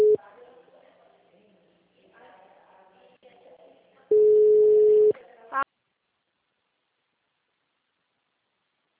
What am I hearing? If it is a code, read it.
Ringback tone (Europe, 425 Hz), with speech.
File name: out- -2007- - - .19015